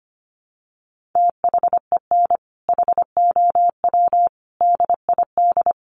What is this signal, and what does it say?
Morse code, heard as a keyed tone, about 25 wpm, 700 Hz. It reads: THEN HOW DID
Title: Sentences from Top 1400 Words 25wpm